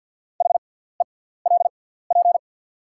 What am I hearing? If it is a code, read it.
Morse code, heard as a keyed tone, about 48 wpm, spaced as for 17 wpm, 690 Hz. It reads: HELP
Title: 6x Character Spacing - Top 200 Words - 50wpm